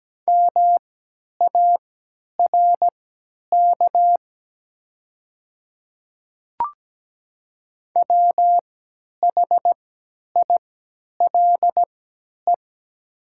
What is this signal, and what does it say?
Morse code, heard as a keyed tone, about 17 wpm, spaced as for 10 wpm, 700 Hz. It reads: MARK WHILE
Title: 3x Character Spacing - Top 300 Words - 17wpm